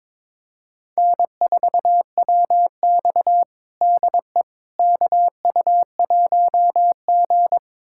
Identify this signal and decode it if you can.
Morse code, heard as a keyed tone, about 22 wpm, 700 Hz. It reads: N4WX DE KU1G
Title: USA Callsign DE Callsign - 4 Characters - 22wpm